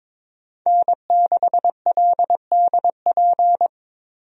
Morse code, heard as a keyed tone, about 22 wpm, 700 Hz. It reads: N6LDP
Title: USA Call Signs - 5 Characters - Repeated 2x - 22wpm